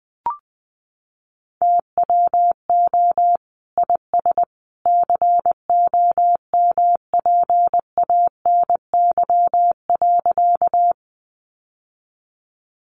Morse code, heard as a keyed tone, about 20 wpm, 700 Hz. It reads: TWO IS COMPANY.
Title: Sentences from Top 600 Words 20wpm